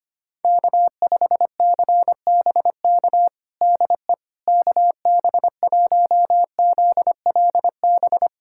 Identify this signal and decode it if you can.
Morse code, heard as a keyed tone, about 25 wpm, 700 Hz. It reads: K5CBK DE KB1ZLB
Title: USA Callsign DE Callsign 25wpm